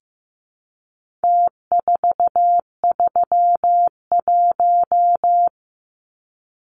Morse code, heard as a keyed tone, about 15 wpm, 700 Hz. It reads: T431
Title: Numbers - Contest Style - 4-Digit 15wpm